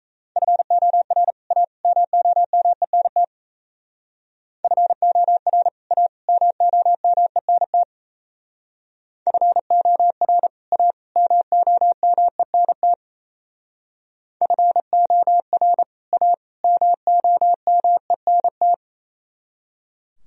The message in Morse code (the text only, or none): FOR A MOMENT FOR A MOMENT FOR A MOMENT FOR A MOMENT